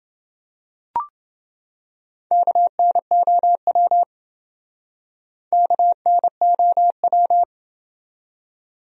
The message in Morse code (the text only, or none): KNOW KNOW